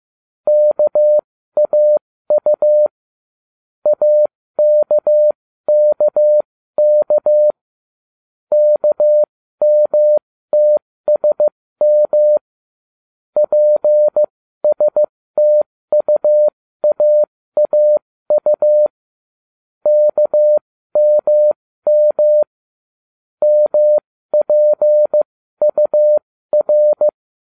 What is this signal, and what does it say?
Morse code, heard as a keyed tone, about 15 wpm, spaced as for 13 wpm, 600 Hz. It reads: KAU AKKK KMTSM PSTUAAU KMM MPUR